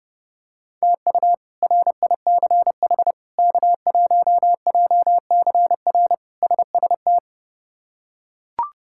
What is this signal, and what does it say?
Morse code, heard as a keyed tone, about 30 wpm, 700 Hz. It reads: TU RICH K1JCR SST
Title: K1USN SST QSO Practice - 30wpm